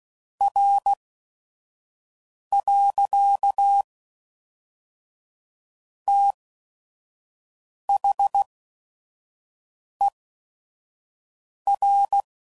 Morse code, heard as a keyed tone, about 16 wpm, spaced as for 5 wpm, 800 Hz. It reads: R. THER
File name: CW Code Groups @5WPM